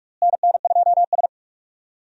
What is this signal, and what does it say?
Morse code, heard as a keyed tone, about 46 wpm, 700 Hz. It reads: NN2S
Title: USA Call Signs - 4 Characters - Repeated 3x - 45wpm